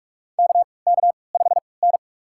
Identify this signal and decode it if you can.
Morse code, heard as a keyed tone, about 45 wpm, spaced as for 25 wpm, 700 Hz. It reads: KK5N
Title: USA Call Signs - 4 Characters - 3x Character Spacing - 45wpm